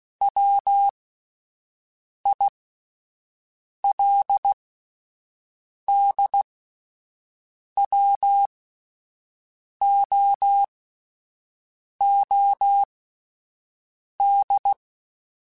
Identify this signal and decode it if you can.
Morse code, heard as a keyed tone, about 16 wpm, spaced as for 5 wpm, 800 Hz. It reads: WILDWOOD